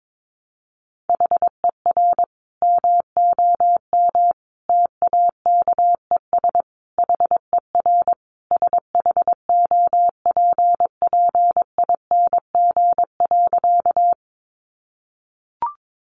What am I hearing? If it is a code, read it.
Morse code, heard as a keyed tone, about 22 wpm, 700 Hz. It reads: HER MOM TAKES HER SHOPPING.
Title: Additional Easy Phrases 22wpm